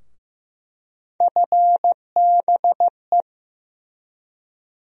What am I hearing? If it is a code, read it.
Morse code, heard as a keyed tone, about 15 wpm, 700 Hz. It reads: FBE
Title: USA Call Sign Postfixes - 3 Letters 15wpm